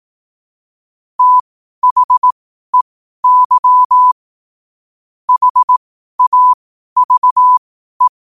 Morse code, heard as a keyed tone, about 18 wpm, spaced as for 12 wpm, 1000 Hz. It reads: THEY HAVE
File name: qotdfiles / 12wpm